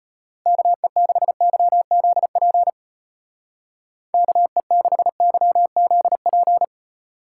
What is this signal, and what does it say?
Morse code, heard as a keyed tone, about 31 wpm, 700 Hz. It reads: KE6YZP KE6YZP
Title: Speed Racing - USA 2x3 Call Signs 25wpm